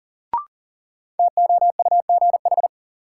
Morse code, heard as a keyed tone, about 40 wpm, 700 Hz. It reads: TOUGH
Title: Sentences from Top 1200 Words - New Word Drill Alt - 40wpm